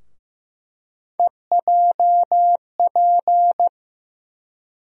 Morse code, heard as a keyed tone, about 15 wpm, 700 Hz. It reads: EJP
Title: USA Call Sign Postfixes - 3 Letters 15wpm